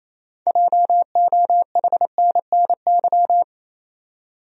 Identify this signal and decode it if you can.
Morse code, heard as a keyed tone, about 28 wpm, 700 Hz. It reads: JOHNNY
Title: USA Common Names - 28wpm